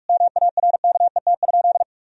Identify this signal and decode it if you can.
Morse code, heard as a keyed tone, about 41 wpm, 700 Hz. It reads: MARKET?
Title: Sentences from Top 1000 Words - Review Part 3 - 45wpm